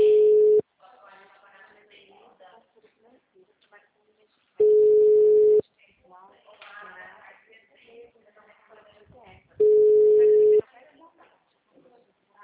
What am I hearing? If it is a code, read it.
Ringback tone (Europe, 425 Hz), with speech.
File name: out- -2016- - - .58777